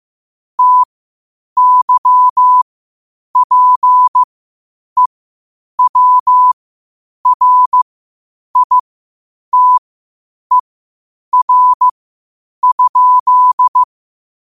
Morse code, heard as a keyed tone, about 15 wpm, spaced as for 8 wpm, 1000 Hz. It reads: TYPEWRITER?
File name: qotdfiles / 8wpm